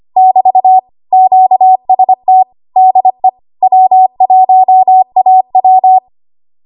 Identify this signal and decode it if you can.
Morse code, heard as a keyed tone, about 25 wpm, 750 Hz. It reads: <BT> QST DE W1AW
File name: Archive / 25 WPM / 25WPM